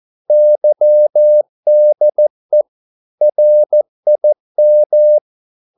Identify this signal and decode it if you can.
Morse code, heard as a keyed tone, about 14 wpm, 600 Hz. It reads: YDE RIM
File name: Korte ord DK 14wpm